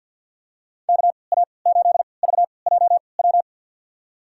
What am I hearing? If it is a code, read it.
Morse code, heard as a keyed tone, about 51 wpm, spaced as for 28 wpm, 700 Hz. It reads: KA8VJW
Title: USA Call Signs - 3x Character Spacing - 50wpm